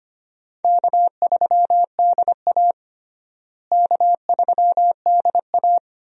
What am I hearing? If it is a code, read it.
Morse code, heard as a keyed tone, about 25 wpm, 700 Hz. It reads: K3DA K3DA
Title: USA Call Signs - 4 Characters - Repeated 3x - 25wpm